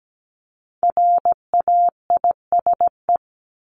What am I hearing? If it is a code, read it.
Morse code, heard as a keyed tone, about 17 wpm, 700 Hz. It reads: RAISE
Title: ED Suffix - Top 100 Words with Base Words 17wpm